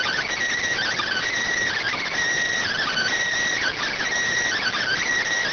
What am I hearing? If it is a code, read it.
Emergency Alert System (SAME) tones (header bursts), heard as harsh buzzing bursts. It is heard from 0.8 to 5.0 s.